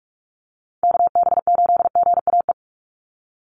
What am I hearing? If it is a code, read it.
Morse code, heard as a keyed tone, about 44 wpm, 700 Hz. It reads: KB8GAE
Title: Call Signs Repeated 2x - 45wpm